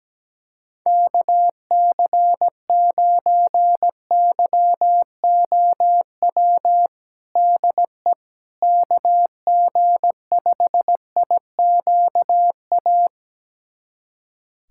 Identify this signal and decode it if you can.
Morse code, heard as a keyed tone, about 17 wpm, 700 Hz. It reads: KC9YOW DE KG5IQA
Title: USA Callsign DE Callsign - 6 Characters - 17wpm